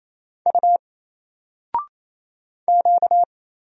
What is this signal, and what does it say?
Morse code, heard as a keyed tone, about 27 wpm, 700 Hz. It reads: U Q
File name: Single-Letter-Number-28wpm